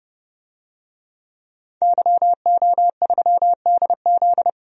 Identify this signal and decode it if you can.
Morse code, heard as a keyed tone, about 32 wpm, 700 Hz. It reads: YO3DZ
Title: DXCC-100 Call Signs - 5 Characters - 30wpm